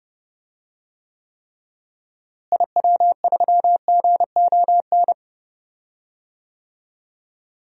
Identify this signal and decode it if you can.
Morse code, heard as a keyed tone, about 31 wpm, 700 Hz. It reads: IW3GON
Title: DXCC-100 Call Signs - 6 Characters - Repeated 3x - 30wpm